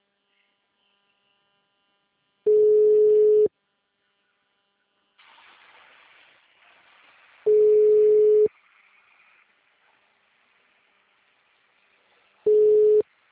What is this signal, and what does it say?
Ringback tone (Europe, 425 Hz).